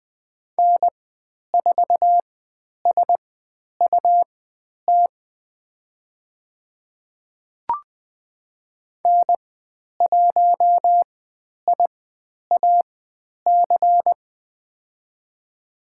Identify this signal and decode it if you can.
Morse code, heard as a keyed tone, about 20 wpm, spaced as for 10 wpm, 700 Hz. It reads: N4SUT N1IAC
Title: 102 Lesson 17g - Letter u - Call Signs 20wpm